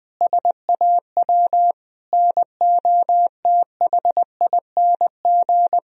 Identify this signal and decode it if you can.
Morse code, heard as a keyed tone, about 20 wpm, 700 Hz. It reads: SAW NOTHING
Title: Sentences from Top 1000 Words - Review Part 3 - 20wpm